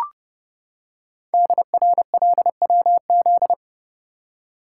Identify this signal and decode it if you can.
Morse code, heard as a keyed tone, about 30 wpm, 700 Hz. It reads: DRLWZ
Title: Code Groups - 5 Letters - 30wpm